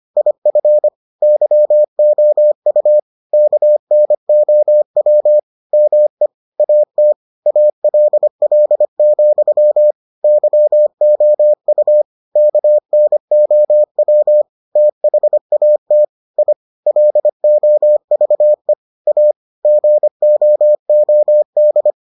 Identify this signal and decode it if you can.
Morse code, heard as a keyed tone, about 25 wpm, 600 Hz. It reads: IF YOU KNOW ME AT ALL, YOU KNOW THAT I LOVE A GOOD